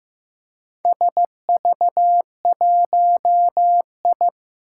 Morse code, heard as a keyed tone, about 15 wpm, 700 Hz. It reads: SV1I
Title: Four Challenging Letter-Number Combinations 15wpm